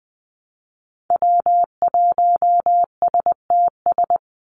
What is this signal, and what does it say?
Morse code, heard as a keyed tone, about 20 wpm, 700 Hz. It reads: W1STS
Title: USA Call Signs 20wpm